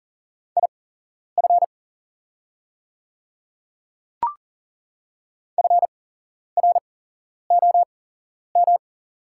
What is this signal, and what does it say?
Morse code, heard as a keyed tone, about 40 wpm, spaced as for 11 wpm, 690 Hz. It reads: IF FROM
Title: 8x Character Spacing - Top 100 Words - 40wpm